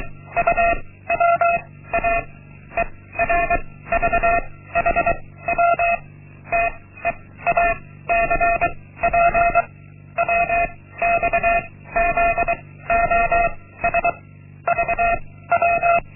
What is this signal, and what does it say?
Morse code, heard as a keyed tone, about 23 wpm, 690 Hz. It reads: U W A E R V H W T E A C P W X Z O S V W
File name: Linksmakalnis-1